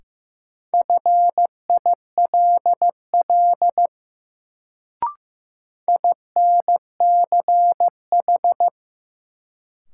Morse code, heard as a keyed tone, about 15 wpm, 700 Hz. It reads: FILL INCH
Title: Sentences from Top 1000 Words - Review Part 3 - New Word Drill - 15wpm